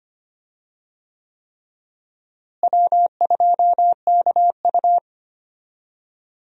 Morse code, heard as a keyed tone, about 25 wpm, 700 Hz. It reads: W2KU
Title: USA Call Signs - 4 Characters - Repeated 3x - 25wpm